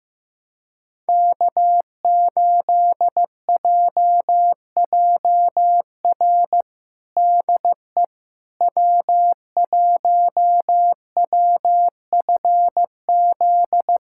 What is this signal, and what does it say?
Morse code, heard as a keyed tone, about 15 wpm, 700 Hz. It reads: K8JJR DE W1WFZ
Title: USA Callsign DE Callsign - 5 Characters - 15wpm